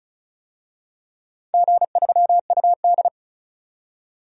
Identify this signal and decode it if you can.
Morse code, heard as a keyed tone, about 35 wpm, 700 Hz. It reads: G3UD
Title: DXCC-100 Call Signs - 4 Characters - 35wpm